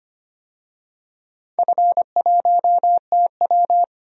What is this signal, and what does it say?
Morse code, heard as a keyed tone, about 25 wpm, 700 Hz. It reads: F1TW
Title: DXCC-100 Call Signs - 4 Characters - 25wpm